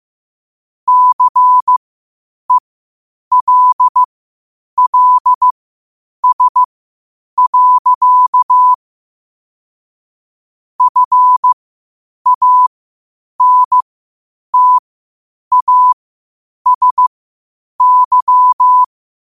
Morse code, heard as a keyed tone, about 15 wpm, spaced as for 8 wpm, 1000 Hz. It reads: CELLS. FANTASY
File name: qotdfiles / 8wpm